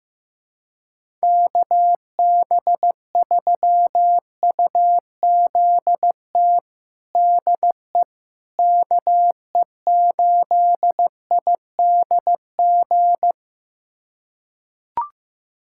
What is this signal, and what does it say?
Morse code, heard as a keyed tone, about 15 wpm, 700 Hz. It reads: KB3UZT DE KE8IDG
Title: USA Callsign DE Callsign 15wpm